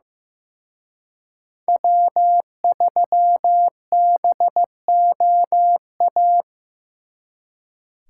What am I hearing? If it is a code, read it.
Morse code, heard as a keyed tone, about 18 wpm, 700 Hz. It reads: W3BOA
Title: Speed Racing - USA 1x3 Call Signs 15wpm